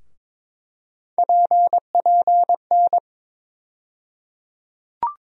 Morse code, heard as a keyed tone, about 21 wpm, 700 Hz. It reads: PPN
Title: USA Call Sign Postfixes - 3 Letters 22wpm